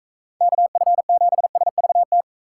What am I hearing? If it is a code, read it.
Morse code, heard as a keyed tone, about 45 wpm, 700 Hz. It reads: KF7SVT